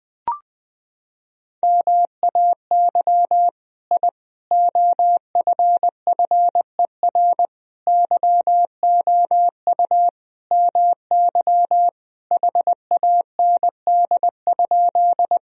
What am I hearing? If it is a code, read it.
Morse code, heard as a keyed tone, about 20 wpm, 700 Hz. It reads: MAY I OFFER YOU MY HAND?
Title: Sentences from Top 1000 Words - Review Part 4 - 20wpm